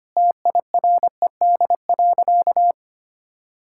Morse code, heard as a keyed tone, about 25 wpm, 700 Hz. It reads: TIRED.
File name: practice / 25wpm / Sentences with 3-Word Phrases - 25wpm